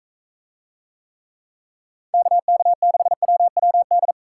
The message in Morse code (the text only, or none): KK6WWD